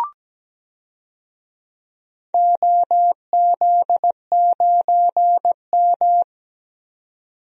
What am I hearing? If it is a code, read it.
Morse code, heard as a keyed tone, about 17 wpm, 700 Hz. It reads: OZ9M
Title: DXCC-100 Call Signs - 4 Characters - 17wpm